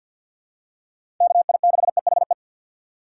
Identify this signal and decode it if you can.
Morse code, heard as a keyed tone, about 48 wpm, 700 Hz. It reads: KI6EHE
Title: USA Call Signs - 6 Characters - 3x Character Spacing - 50wpm